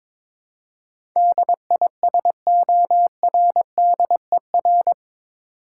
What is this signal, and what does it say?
Morse code, heard as a keyed tone, about 22 wpm, 700 Hz. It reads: DISORDER
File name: practice / 22wpm / Eight Letter Words - 22wpm